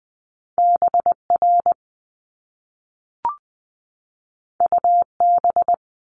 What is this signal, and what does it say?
Morse code, heard as a keyed tone, about 20 wpm, 700 Hz. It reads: BR UB
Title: Top Two Letter Combinations in Words 20wpm